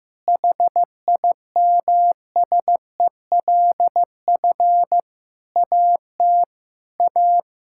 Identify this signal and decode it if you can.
Morse code, heard as a keyed tone, about 15 wpm, 700 Hz. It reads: HIMSELF AT A
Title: Sentences from Top 1600 Words 15wpm